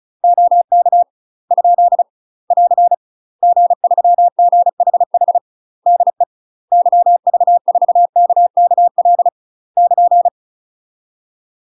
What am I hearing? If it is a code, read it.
Morse code, heard as a keyed tone, about 34 wpm, 700 Hz. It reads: OK ? <AR> G3GHH DE YV4KKL <KN>
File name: UK MASTER QSOs TRAINING / 35WPM / UK MASTER QSOs 0029